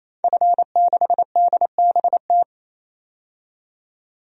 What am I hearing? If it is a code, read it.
Morse code, heard as a keyed tone, about 27 wpm, 700 Hz. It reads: F6DBT
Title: DXCC-100 Call Signs - 5 Characters - 28wpm